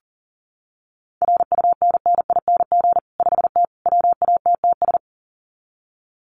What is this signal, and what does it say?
Morse code, heard as a keyed tone, about 40 wpm, 700 Hz. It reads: RUNNING 5T WATTS
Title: Common QSO Phrases 40wpm